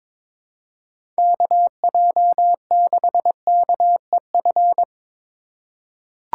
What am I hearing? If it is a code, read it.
Morse code, heard as a keyed tone, about 21 wpm, 700 Hz. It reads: KJ6KEF
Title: USA Call Signs - 6 Characters - 2x Character Spacing - 22wpm